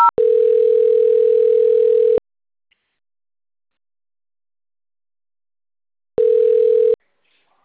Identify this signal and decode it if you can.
Ringback tone (North America), 440 + 480 Hz.
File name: out- - - - - .44946